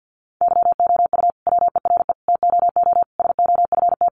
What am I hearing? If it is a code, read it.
Morse code, heard as a keyed tone, about 50 wpm, 700 Hz. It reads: YOU WERE TOO SOFT
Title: Sentences from Top 1500 Words 50wpm